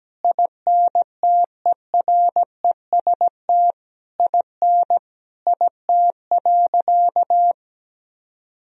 Morse code, heard as a keyed tone, about 17 wpm, 700 Hz. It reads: INTEREST IN IT.
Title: Sentences from Top 1000 Words - Review Part 2 - 17wpm